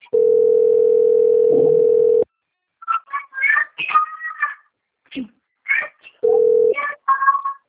Ringback tone (North America), 440 + 480 Hz.
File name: out- - - - - .13949